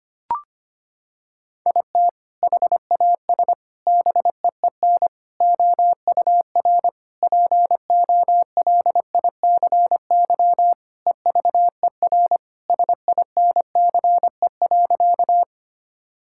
Morse code, heard as a keyed tone, about 25 wpm, 700 Hz. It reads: IT HAS BEEN OUR POLICY EVER SINCE.
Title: Sentences from Top 1000 Words - Review Part 5 - 25wpm